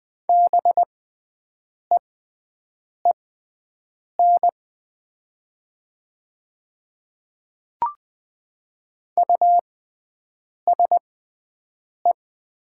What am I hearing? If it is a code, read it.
Morse code, heard as a keyed tone, about 20 wpm, spaced as for 7 wpm, 700 Hz. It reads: BEEN USE
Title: 6x Character Spacing - Top 100 Words - 20wpm